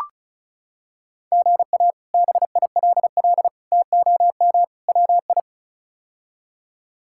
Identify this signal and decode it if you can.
Morse code, heard as a keyed tone, about 36 wpm, 700 Hz. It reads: GA BILL TOM WI
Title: K1USN SST QSO Practice - 35wpm